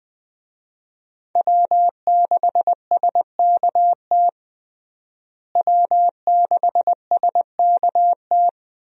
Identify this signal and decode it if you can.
Morse code, heard as a keyed tone, about 20 wpm, 700 Hz. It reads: W6SKT W6SKT
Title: USA Call Signs - 5 Characters - Repeated 2x - 20wpm